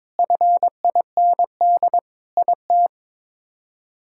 Morse code, heard as a keyed tone, about 22 wpm, 700 Hz. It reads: FIND IT
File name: practice / 22wpm / Sentences with 3-Word Phrases - 22wpm